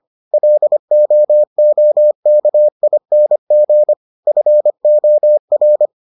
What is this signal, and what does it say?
Morse code, heard as a keyed tone, about 25 wpm, 600 Hz. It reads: LOOKING FOR